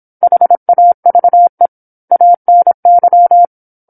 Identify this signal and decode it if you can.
Morse code, heard as a keyed tone, about 27 wpm, 700 Hz. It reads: HAVE ANY
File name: 08-Chapter-08 26 WPM